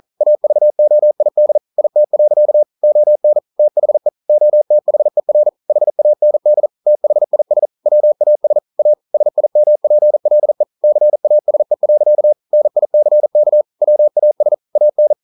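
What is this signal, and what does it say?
Morse code, heard as a keyed tone, about 40 wpm, 600 Hz. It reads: AVOID IT. ON THE OTHER HAND THIS WAS A SIMPLE CASE. NICK WAS AN
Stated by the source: Morse code practice at 41 wpm, from CW med Gnister.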